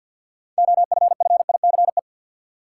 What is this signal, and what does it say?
Morse code, heard as a keyed tone, about 52 wpm, 700 Hz. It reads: OFFICE
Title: Six Letter Words - 50wpm